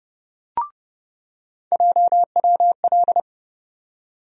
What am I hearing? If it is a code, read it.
Morse code, heard as a keyed tone, about 30 wpm, 700 Hz. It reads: JWL
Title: USA Call Sign Postfixes - 3 Letters 30wpm